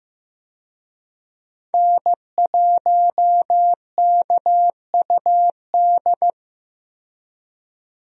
Morse code, heard as a keyed tone, about 16 wpm, 700 Hz. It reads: N1KUD
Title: USA Call Signs - 5 Characters - 15wpm